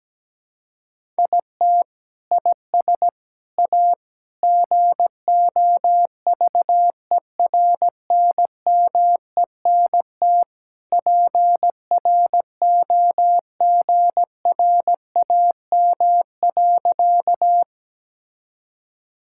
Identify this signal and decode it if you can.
Morse code, heard as a keyed tone, about 17 wpm, 700 Hz. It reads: IT IS A GOVERNMENT PROGRAM.
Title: Sentences from Top 1000 Words - Review Part 5 - 17wpm